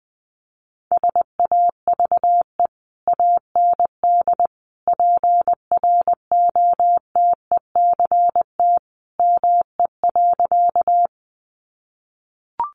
Morse code, heard as a keyed tone, about 20 wpm, 700 Hz. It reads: SAVE AND PROTECT ME.
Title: Sentences from Top 1100 Words 20wpm